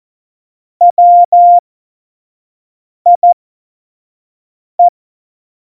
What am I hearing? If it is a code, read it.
Morse code, heard as a keyed tone, about 14 wpm, spaced as for 5 wpm, 700 Hz. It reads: WIE
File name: Morsekurs / Witze / Witz 24 April 2018, 25 F70